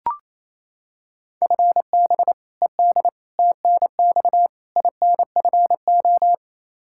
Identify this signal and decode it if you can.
Morse code, heard as a keyed tone, about 28 wpm, 700 Hz. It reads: FB ED TNX INFO
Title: Practice QSO - Exchange 3 - More Info - 28wpm